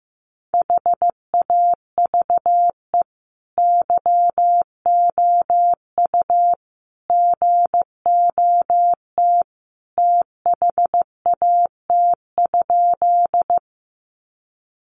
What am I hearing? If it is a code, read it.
Morse code, heard as a keyed tone, about 15 wpm, 700 Hz. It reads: HAVE YOU GOT THAT?
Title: Sentences from Top 300 Words 15wpm